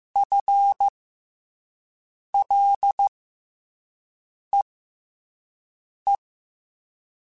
Morse code, heard as a keyed tone, about 15 wpm, spaced as for 5 wpm, 800 Hz. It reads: FLEE
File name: FL / FL mp3s / FL Prefixes 3x2 Twice @5